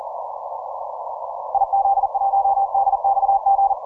Morse code, heard as a keyed tone, about 39 wpm, 790 Hz. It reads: IZ1DXG
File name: soundclips / TN5R 10MCW